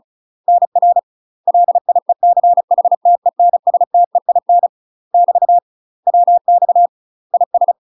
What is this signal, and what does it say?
Morse code, heard as a keyed tone, about 36 wpm, 700 Hz. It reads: NR LIECHTENSTEIN <BT> WX IS
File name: UK MASTER QSOs TRAINING / 35WPM / UK MASTER QSOs 0010